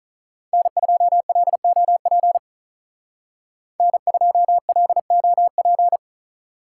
Morse code, heard as a keyed tone, about 47 wpm, 700 Hz. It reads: N2LOP N2LOP